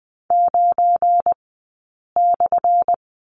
Morse code, heard as a keyed tone, about 19 wpm, spaced as for 8 wpm, 700 Hz. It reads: 9/
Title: 177 Lesson 29d - Slash - Two Characters 20wpm